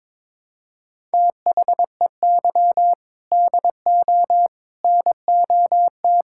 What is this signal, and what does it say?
Morse code, heard as a keyed tone, about 22 wpm, 700 Hz. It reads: THEY DO NOT
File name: practice / 22wpm / Build A Phrase - 5-Word Phrases - 22wpm